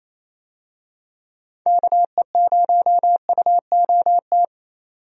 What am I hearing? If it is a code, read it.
Morse code, heard as a keyed tone, about 27 wpm, 700 Hz. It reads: KE0UOT
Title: USA Call Signs - 6 Characters - 28wpm